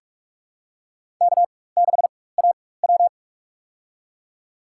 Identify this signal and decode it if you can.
Morse code, heard as a keyed tone, about 46 wpm, spaced as for 21 wpm, 700 Hz. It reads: K6AW
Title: USA Call Signs - 4 Characters - 4x Character Spacing - 45wpm